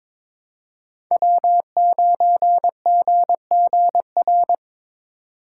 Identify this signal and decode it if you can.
Morse code, heard as a keyed tone, about 22 wpm, 700 Hz. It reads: W9GGR
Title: USA Call Signs - 2x Character Spacing - 22wpm